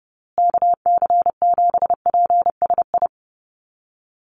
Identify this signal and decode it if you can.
Morse code, heard as a keyed tone, about 30 wpm, 700 Hz. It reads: KC7PSI
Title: USA Call Signs Repeated 3x 30wpm